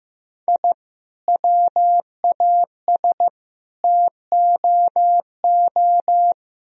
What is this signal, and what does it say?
Morse code, heard as a keyed tone, about 15 wpm, 700 Hz. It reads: I WAS TOO
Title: Sentences from Top 1000 Words - Review Part 4 - 15wpm